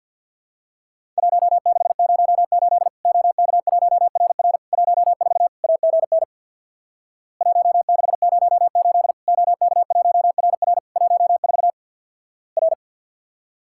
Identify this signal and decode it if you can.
Morse code, heard as a keyed tone, about 50 wpm, 700 Hz. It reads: 1608 OK1RR 14 AGN 1608 OK1RR 14 R
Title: Worked All Europe DX Contest - QTC Exchange with Repeats - 50wpm